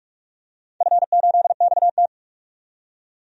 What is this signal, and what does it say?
Morse code, heard as a keyed tone, about 43 wpm, 700 Hz. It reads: F8XT
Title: DXCC-100 Call Signs - 4 Characters - 45wpm